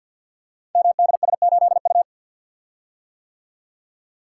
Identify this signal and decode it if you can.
Morse code, heard as a keyed tone, about 50 wpm, 700 Hz. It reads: MDS8U
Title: Code Groups - 5 Letter:Numbers - 50wpm